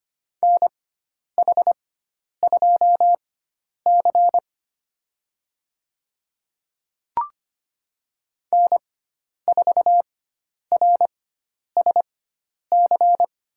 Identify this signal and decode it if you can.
Morse code, heard as a keyed tone, about 25 wpm, spaced as for 10 wpm, 700 Hz. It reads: NH2C N4RSC
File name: course / 25:10wpm / 095 Lesson 16g - Letter c - Call Signs 25wpm